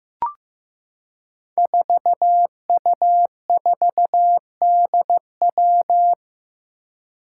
Morse code, heard as a keyed tone, about 15 wpm, 700 Hz. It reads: 4U4DW
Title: Five Challenging Letter-Number Combinations 15wpm